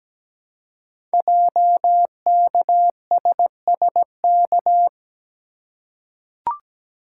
Morse code, heard as a keyed tone, about 17 wpm, 700 Hz. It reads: JKSSK
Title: Code Groups - 5 Letter:Numbers - 17wpm